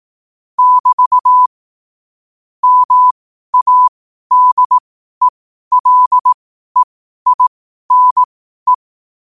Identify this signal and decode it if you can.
Morse code, heard as a keyed tone, about 18 wpm, spaced as for 12 wpm, 1000 Hz. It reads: <BT> MADELEINE
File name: qotdfiles / 12wpm